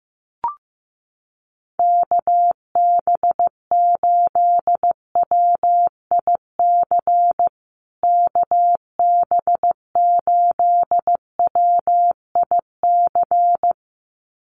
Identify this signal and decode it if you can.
Morse code, heard as a keyed tone, about 15 wpm, 700 Hz. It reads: KB8WIC KB8WIC
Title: USA Call Signs Repeated 2x 15wpm